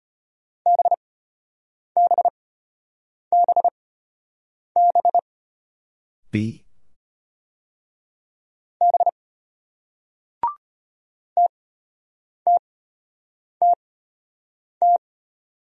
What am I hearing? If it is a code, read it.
Morse code, heard as a keyed tone, about 34 wpm, spaced as for 8 wpm, 700 Hz. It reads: BBBB B TTTT